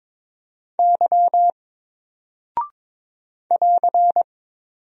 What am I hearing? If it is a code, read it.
Morse code, heard as a keyed tone, about 22 wpm, 700 Hz. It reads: Y <AR>
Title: Single Character (Letter:Number:Punctuation:Prosign) 22wpm